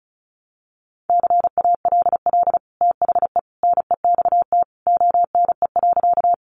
Morse code, heard as a keyed tone, about 34 wpm, 700 Hz. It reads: CALL THE NEXT ONE.
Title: Sentences from Top 300 Words 35wpm